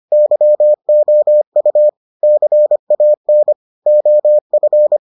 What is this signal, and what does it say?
Morse code, heard as a keyed tone, about 25 wpm, 600 Hz. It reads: YOU CAN OF